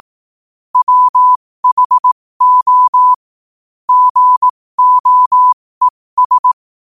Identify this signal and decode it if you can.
Morse code, heard as a keyed tone, about 18 wpm, spaced as for 15 wpm, 1000 Hz. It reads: WHO GOES